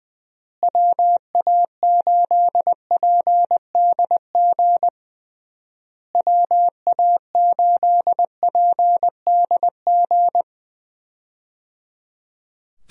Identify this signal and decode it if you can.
Morse code, heard as a keyed tone, about 21 wpm, 700 Hz. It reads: WA8PDG WA8PDG
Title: USA Call Signs - 6 Characters - Repeated 3x - 20wpm